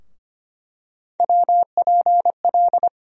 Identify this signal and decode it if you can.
Morse code, heard as a keyed tone, about 25 wpm, 700 Hz. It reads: WPL
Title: USA Call Sign Postfixes - 3 Letters 25wpm